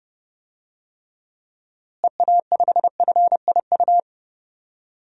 Morse code, heard as a keyed tone, about 30 wpm, 700 Hz. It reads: EA5FIU
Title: DXCC-100 Call Signs - 6 Characters - Repeated 2x - 30wpm